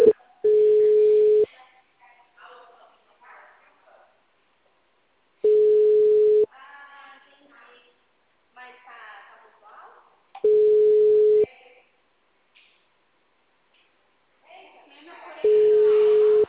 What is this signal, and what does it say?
Ringback tone (Europe, 425 Hz), with speech.